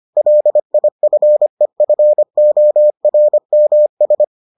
Morse code, heard as a keyed tone, about 25 wpm, 600 Hz. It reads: LIFEFORMS